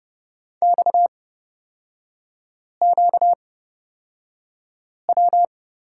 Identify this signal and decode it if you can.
Morse code, heard as a keyed tone, about 30 wpm, spaced as for 5 wpm, 700 Hz. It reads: XQW
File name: Lesson 38e - Letter q - Three Characters 30wpm